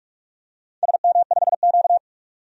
Morse code, heard as a keyed tone, about 44 wpm, 700 Hz. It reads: SM5Q
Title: DXCC-100 Call Signs - 4 Characters - 45wpm